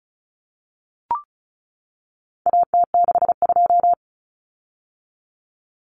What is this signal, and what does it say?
Morse code, heard as a keyed tone, about 34 wpm, 700 Hz. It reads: AT62
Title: Common Cut Numbers - 4-Digit 35wpm